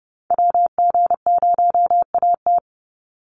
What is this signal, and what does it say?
Morse code, heard as a keyed tone, about 30 wpm, 700 Hz. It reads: WG0AT
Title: NA SOTA Chaser Callsigns 30wpm